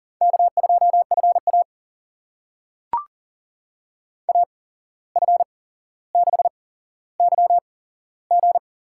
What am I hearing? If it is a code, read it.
Morse code, heard as a keyed tone, about 40 wpm, spaced as for 16 wpm, 700 Hz. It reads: K2FA AF6YG
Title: USA Call Signs - 8x Character Spacing - 40wpm